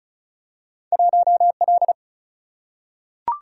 Morse code, heard as a keyed tone, about 36 wpm, 700 Hz. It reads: 1L
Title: Single-Letter Single-Number 35wpm